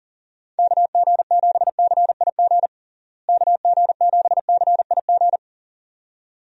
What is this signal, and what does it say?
Morse code, heard as a keyed tone, about 40 wpm, 700 Hz. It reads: KG7CIG KG7CIG
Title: USA Call Signs - 6 Characters - Repeated 2x - 40wpm